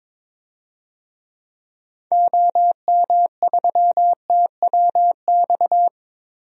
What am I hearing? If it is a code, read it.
Morse code, heard as a keyed tone, about 22 wpm, 700 Hz. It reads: OM3TWX
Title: DXCC-100 Call Signs - 6 Characters - 22wpm